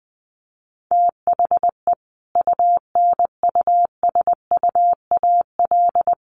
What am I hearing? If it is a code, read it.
Morse code, heard as a keyed tone, about 20 wpm, 700 Hz. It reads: THE UNUSUAL
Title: Sentences from Top 1400 Words 20wpm